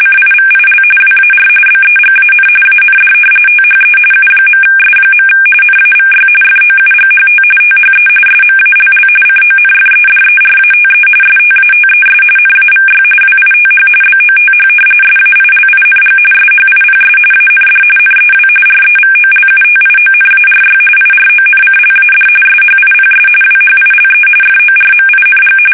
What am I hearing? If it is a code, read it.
RTTY, about 45.45 baud. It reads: CIVIL WAR BEGAN AND THE U.S. NAVY FOUGHT THE SMALL CONFEDERATE STATES NAVY WITH BOTH SAILING SHIPS AND NEW REVOLUTIONARY IRONCLAD SHIPS WHILE FORMING
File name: fsk850 / usnavy1